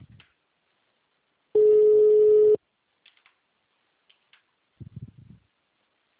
Ringback tone (Europe, 425 Hz).
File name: out- -2011- - - .40408